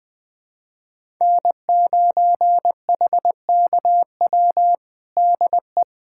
Morse code, heard as a keyed tone, about 20 wpm, 700 Hz. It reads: N9HKW DE
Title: USA Callsign DE Callsign - 5 Characters - 20wpm